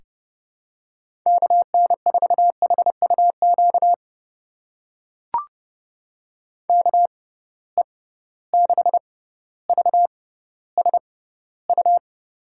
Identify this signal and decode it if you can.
Morse code, heard as a keyed tone, about 30 wpm, spaced as for 15 wpm, 700 Hz. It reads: KN4HUQ KE6VSU
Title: USA Call Signs - 6 Characters - 6x Character Spacing - 30wpm